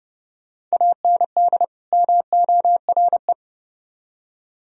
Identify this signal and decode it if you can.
Morse code, heard as a keyed tone, about 38 wpm, 700 Hz. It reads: AND MORE